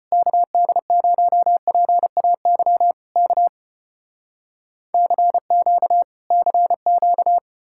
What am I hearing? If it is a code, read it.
Morse code, heard as a keyed tone, about 34 wpm, 700 Hz. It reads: KD0PAY K CQ CQ